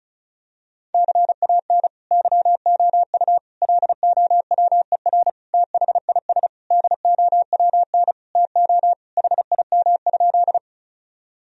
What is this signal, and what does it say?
Morse code, heard as a keyed tone, about 36 wpm, 700 Hz. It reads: CAN YOU LOWER THIS DOWN TO HIM?
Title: Sentences from Top 1000 Words - Review Part 4 - 35wpm